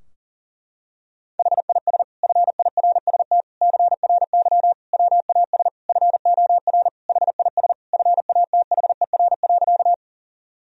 Morse code, heard as a keyed tone, about 40 wpm, 700 Hz. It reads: HIS FIRST CRY WAS FOR HIS FATHER.
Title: Sentences from Top 1000 Words - Review Part 2 - 40wpm